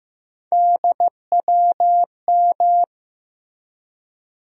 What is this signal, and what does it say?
Morse code, heard as a keyed tone, about 15 wpm, 700 Hz. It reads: DWM
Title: USA Call Sign Postfixes - 3 Letters 15wpm